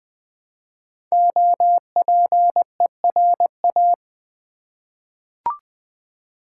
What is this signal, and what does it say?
Morse code, heard as a keyed tone, about 20 wpm, 700 Hz. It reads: OPERA
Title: 5 Letter Words - First 3 Letters Given - 20wpm